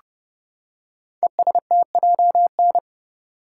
Morse code, heard as a keyed tone, about 30 wpm, 700 Hz. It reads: ESTJN
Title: Code Groups - 5 Letter:Numbers - 30wpm